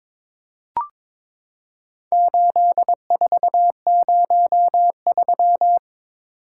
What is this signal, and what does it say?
Morse code, heard as a keyed tone, about 21 wpm, 700 Hz. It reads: 8403
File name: practice / 22wpm / Four Letter-Number Combinations - 22wpm